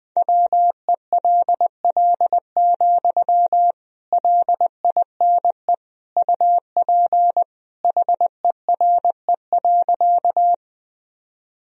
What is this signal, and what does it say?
Morse code, heard as a keyed tone, about 20 wpm, 700 Hz. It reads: WELL, LINE UP HERE.
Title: Sentences from Top 1000 Words - Review Part 1 - 20wpm